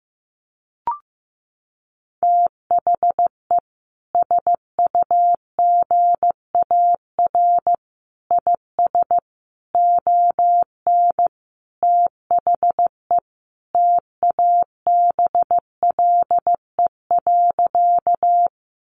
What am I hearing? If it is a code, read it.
Morse code, heard as a keyed tone, about 15 wpm, 700 Hz. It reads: THE SUGAR IS ON THE TABLE.
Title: Sentences from Top 1200 Words 15wpm